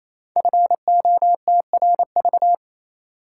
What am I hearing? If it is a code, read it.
Morse code, heard as a keyed tone, about 27 wpm, 700 Hz. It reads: FOTRV
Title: Code Groups - 5 Letters - 28wpm